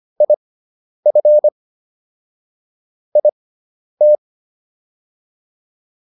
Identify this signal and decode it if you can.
Morse code, heard as a keyed tone, about 25 wpm, spaced as for 10 wpm, 600 Hz. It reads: IF IT